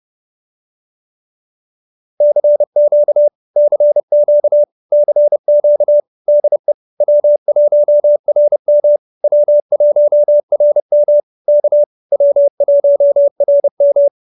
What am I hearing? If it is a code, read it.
Morse code, heard as a keyed tone, about 30 wpm, 590 Hz. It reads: CQ CQ CQ DE W1RM W1RM K W1RM
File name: QSO308 30